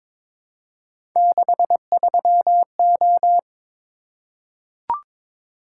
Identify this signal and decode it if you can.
Morse code, heard as a keyed tone, about 22 wpm, 700 Hz. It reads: 63O
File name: practice / 22wpm / Three Letter-Number Combinations - 22wpm